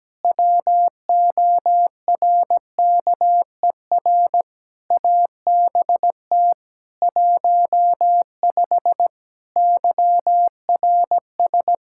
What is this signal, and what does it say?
Morse code, heard as a keyed tone, about 17 wpm, 700 Hz. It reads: WORKER ABT 15 YRS
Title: K1USN SST QSO Practice 17wpm